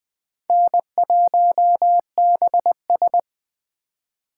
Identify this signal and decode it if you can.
Morse code, heard as a keyed tone, about 20 wpm, 700 Hz. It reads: N1BS
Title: 2023 POTA Activator Call Signs - 20wpm